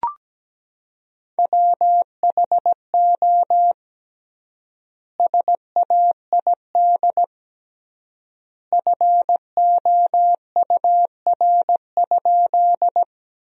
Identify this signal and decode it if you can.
Morse code, heard as a keyed tone, about 17 wpm, 700 Hz. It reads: WHO SAID FOUR?
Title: 3x Word-Spacing - Sentences from Top 300 Words - 17wpm